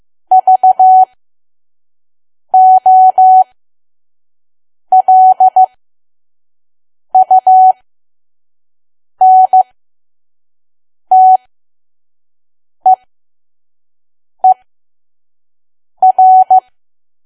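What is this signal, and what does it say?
Morse code, heard as a keyed tone, about 15 wpm, spaced as for 5 wpm, 750 Hz. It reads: VOLUNTEER
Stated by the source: Morse code practice at 5 WPM from KQ2P.COM.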